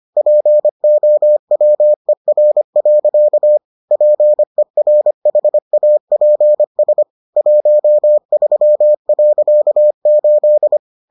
Morse code, heard as a keyed tone, about 25 wpm, 600 Hz. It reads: POWER. PERHAPS 13.8